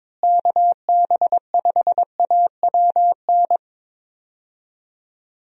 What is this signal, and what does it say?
Morse code, heard as a keyed tone, about 22 wpm, 700 Hz. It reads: KB5AWN
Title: USA Call Signs - 6 Characters - Repeated 3x - 22wpm